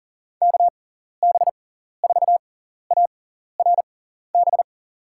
Morse code, heard as a keyed tone, about 40 wpm, spaced as for 14 wpm, 700 Hz. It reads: KB4ARB
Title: USA Call Signs - 6 Characters - 6x Character Spacing - 40wpm